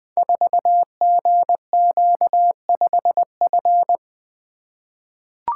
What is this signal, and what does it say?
Morse code, heard as a keyed tone, about 21 wpm, 700 Hz. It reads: 4GQ5F
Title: Code Groups - 5 Letter:Numbers - 20wpm